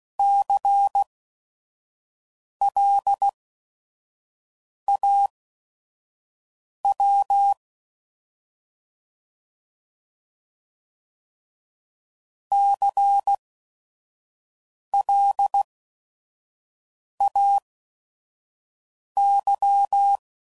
Morse code, heard as a keyed tone, about 16 wpm, spaced as for 5 wpm, 800 Hz. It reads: CLAW CLAY
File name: Prefixes / By First Letter / Cs